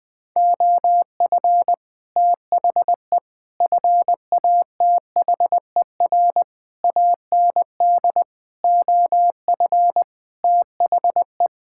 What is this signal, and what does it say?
Morse code, heard as a keyed tone, about 19 wpm, 700 Hz. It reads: OF THE FATHER AND OF THE
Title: Sets of 9 Words - 20wpm